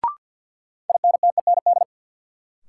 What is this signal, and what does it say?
Morse code, heard as a keyed tone, about 53 wpm, 700 Hz. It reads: INTEND
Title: Sentences from Top 1600 Words - New Word Drill Alt - 50wpm